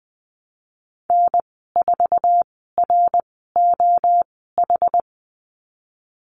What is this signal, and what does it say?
Morse code, heard as a keyed tone, about 20 wpm, 700 Hz. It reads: N 4 R O H
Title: 062 Lesson 11g - Letter h - Call Signs 20wpm